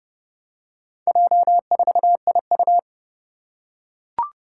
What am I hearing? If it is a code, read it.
Morse code, heard as a keyed tone, about 29 wpm, 690 Hz. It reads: J4IU
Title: Four Challenging Letter-Number Combinations 30wpm